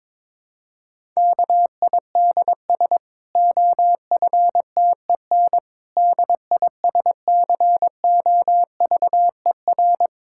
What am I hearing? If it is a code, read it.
Morse code, heard as a keyed tone, about 21 wpm, 700 Hz. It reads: KIDS OFTEN DISCOVER
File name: Difficult Sentences with Hints - 22wpm